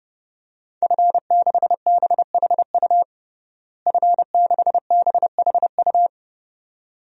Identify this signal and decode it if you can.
Morse code, heard as a keyed tone, about 32 wpm, 700 Hz. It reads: F6BHU F6BHU
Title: DXCC-100 Call Signs - 5 Characters - Repeated 2x - 30wpm